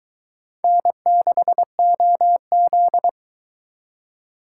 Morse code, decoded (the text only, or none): N6OZ